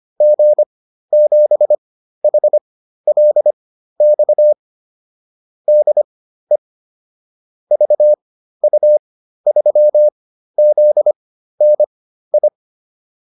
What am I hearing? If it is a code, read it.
Morse code, heard as a keyed tone, about 25 wpm, spaced as for 13 wpm, 600 Hz. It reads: G7HLX DE VU3ZNI